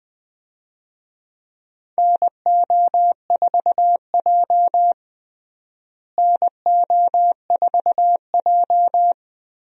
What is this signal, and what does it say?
Morse code, heard as a keyed tone, about 20 wpm, 700 Hz. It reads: NO4J NO4J
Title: USA Call Signs - 4 Characters - Repeated 3x - 20wpm